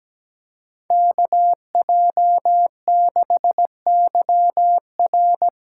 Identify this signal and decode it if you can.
Morse code, heard as a keyed tone, about 17 wpm, 700 Hz. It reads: KJ6YR
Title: USA Call Signs - 6 Characters - 8x Character Spacing - 17wpm